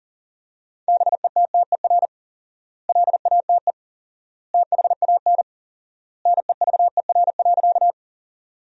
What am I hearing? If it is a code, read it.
Morse code, heard as a keyed tone, about 37 wpm, 700 Hz. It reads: BETTER LATE THAN NEVER.
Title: 4x Word-Spacing - Sentences from Top 400 Words - 40wpm